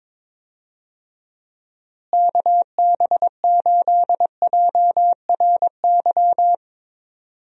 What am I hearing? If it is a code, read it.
Morse code, heard as a keyed tone, about 22 wpm, 700 Hz. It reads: KB8JRY
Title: USA Call Signs - 6 Characters - Repeated 2x - 22wpm